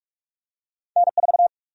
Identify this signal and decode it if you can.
Morse code, heard as a keyed tone, about 46 wpm, 700 Hz. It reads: N4
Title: USA Call Sign Prefixes - 45wpm